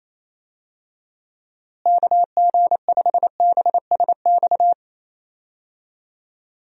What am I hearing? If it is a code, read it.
Morse code, heard as a keyed tone, about 28 wpm, 700 Hz. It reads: KG5BSX
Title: USA Call Signs - 6 Characters - Repeated 3x - 28wpm